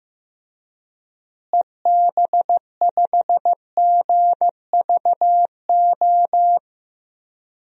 Morse code, heard as a keyed tone, about 15 wpm, 700 Hz. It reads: EB5GVO
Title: DXCC-100 Call Signs - 6 Characters - Repeated 3x - 15wpm